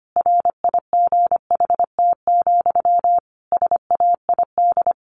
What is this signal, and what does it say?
Morse code, heard as a keyed tone, about 25 wpm, 700 Hz. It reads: RIGHT, SAID
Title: A Sound Of Thunder 25wpm